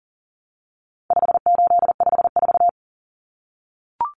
Morse code, heard as a keyed tone, about 40 wpm, 700 Hz. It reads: 5854
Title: Common Cut Numbers - 4-Digit 40wpm